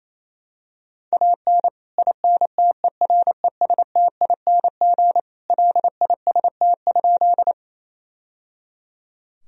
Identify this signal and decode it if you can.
Morse code, heard as a keyed tone, about 28 wpm, 700 Hz. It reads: AN INTERESTING LIST?
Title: Sentences from Top 1000 Words - Review Part 2 - 28wpm